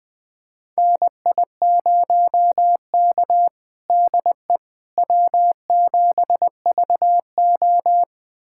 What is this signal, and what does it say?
Morse code, heard as a keyed tone, about 21 wpm, 700 Hz. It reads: NI0K DE W7VO
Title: USA Callsign DE Callsign - 4 Characters - 20wpm